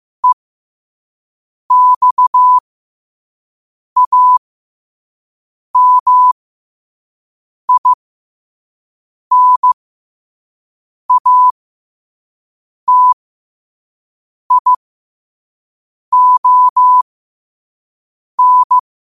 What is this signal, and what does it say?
Morse code, heard as a keyed tone, about 15 wpm, spaced as for 5 wpm, 1000 Hz. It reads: EXAMINATION
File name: qotdfiles / 5wpm